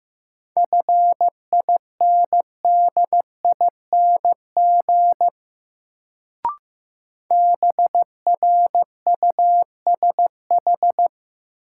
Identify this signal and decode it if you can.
Morse code, heard as a keyed tone, about 15 wpm, 700 Hz. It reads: FINDING BRUSH
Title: Sentences from Top 1100 Words - New Word Drill - 15wpm